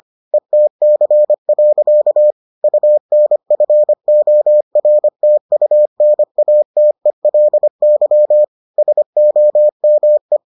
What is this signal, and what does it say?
Morse code, heard as a keyed tone, about 25 wpm, 600 Hz. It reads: ETC. UNFORTUNATELY SOME